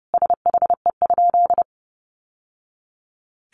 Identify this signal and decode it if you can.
Morse code, heard as a keyed tone, about 30 wpm, 700 Hz. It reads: SHE?
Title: Sentences from Top 1200 Words 30wpm